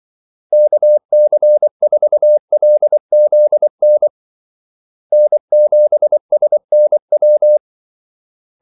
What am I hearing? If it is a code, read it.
Morse code, heard as a keyed tone, about 24 wpm, 600 Hz. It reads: KC4LZN N7SNW
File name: Callsigns-24wpm